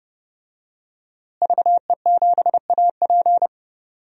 Morse code, heard as a keyed tone, about 30 wpm, 700 Hz. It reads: VE7AP
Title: DXCC-100 Call Signs - 5 Characters - Repeated 3x - 30wpm